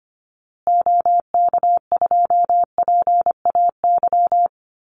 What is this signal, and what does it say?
Morse code, heard as a keyed tone, about 25 wpm, 700 Hz. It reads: OK2PAY
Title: Call Signs Repeated 3x - 25wpm